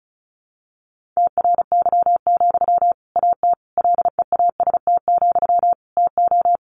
Morse code, heard as a keyed tone, about 34 wpm, 700 Hz. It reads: TRY, AT LEAST, TO
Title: Sentences from Top 700 Words 35wpm